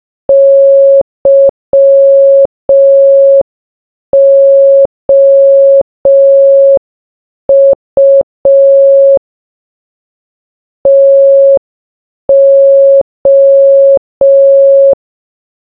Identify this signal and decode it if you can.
Morse code, heard as a keyed tone, about 5 wpm, 550 Hz. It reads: YOU TO
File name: baba-is-you-the-folder-megapack-linux-macos-windows-gamerip-2024 / dobyxyjbym / 30. Morse